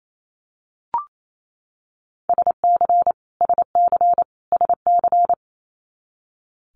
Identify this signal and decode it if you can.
Morse code, heard as a keyed tone, about 28 wpm, 700 Hz. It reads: SC SC SC
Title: Multi-letter Phonemes Repeated 3x 28wpm